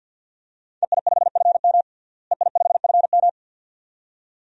Morse code, heard as a keyed tone, about 50 wpm, 700 Hz. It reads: EI5FM EI5FM
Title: DXCC-100 Call Signs - 5 Characters - Repeated 2x - 50wpm